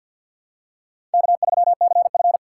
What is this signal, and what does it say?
Morse code, heard as a keyed tone, about 53 wpm, 700 Hz. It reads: K3CF
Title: USA Call Signs - 4 Characters - 6x Character Spacing - 50wpm